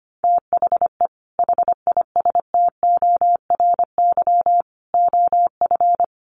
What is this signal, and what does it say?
Morse code, heard as a keyed tone, about 25 wpm, 700 Hz. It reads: THE HISTORY OF
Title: Sets of 4 Words 25wpm